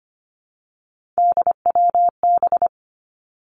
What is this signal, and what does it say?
Morse code, heard as a keyed tone, about 25 wpm, 700 Hz. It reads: DWB
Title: Three Challenging Letter-Number Combinations 25wpm